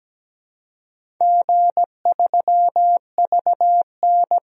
Morse code, heard as a keyed tone, about 17 wpm, 700 Hz. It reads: G3VN
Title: DXCC-100 Call Signs - 4 Characters - Repeated 2x - 17wpm